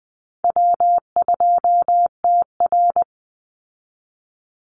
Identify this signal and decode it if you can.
Morse code, heard as a keyed tone, about 20 wpm, 700 Hz. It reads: W2TR
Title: Easy Call Signs Repeated 2x - 20wpm